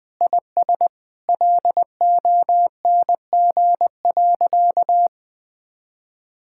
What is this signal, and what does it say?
Morse code, heard as a keyed tone, about 20 wpm, 700 Hz. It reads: IS LONG.
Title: Sentences with 3-Word Phrases - 20wpm